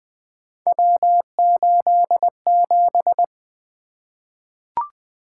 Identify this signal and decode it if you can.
Morse code, heard as a keyed tone, about 20 wpm, 700 Hz. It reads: W87
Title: Three Letter-Number Combinations - 20wpm